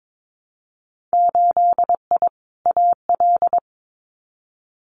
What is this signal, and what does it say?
Morse code, heard as a keyed tone, about 22 wpm, 700 Hz. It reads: 8I AL
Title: Winter Field Day Exchanges Encore 22wpm